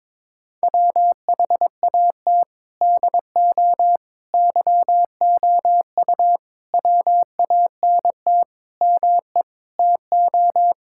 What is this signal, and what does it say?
Morse code, heard as a keyed tone, about 22 wpm, 700 Hz. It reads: WHAT DO YOU WANT ME TO
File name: practice / 22wpm / Sets of 7 Words - 22wpm